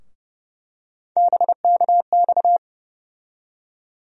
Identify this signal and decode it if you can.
Morse code, heard as a keyed tone, about 30 wpm, 700 Hz. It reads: BKX
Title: USA Call Sign Postfixes - 3 Letters 30wpm